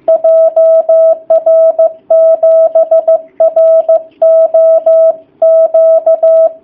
Morse code, heard as a keyed tone, about 15 wpm, 660 Hz. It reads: JR7ROQ
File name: jh8cbh / 2019cw / sp15- w